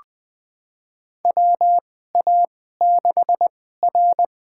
Morse code, heard as a keyed tone, about 20 wpm, spaced as for 14 wpm, 700 Hz. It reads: WA6R
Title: USA Call Signs - 2x Character Spacing - 20wpm